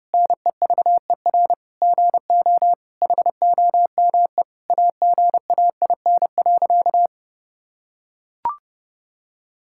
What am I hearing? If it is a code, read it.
Morse code, heard as a keyed tone, about 30 wpm, 700 Hz. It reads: NEVER GO HOME AGAIN.
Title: Sentences from Top 1000 Words - Review Part 6 - 30wpm